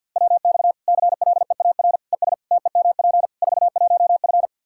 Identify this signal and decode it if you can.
Morse code, heard as a keyed tone, about 49 wpm, 700 Hz. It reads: WX CLEAR ES TEMP 41F
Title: K1USN SST QSO Practice 50wpm